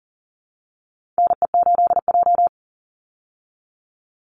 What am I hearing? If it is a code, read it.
Morse code, heard as a keyed tone, about 40 wpm, 700 Hz. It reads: NE8J
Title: Call Signs Repeated 3x - 40wpm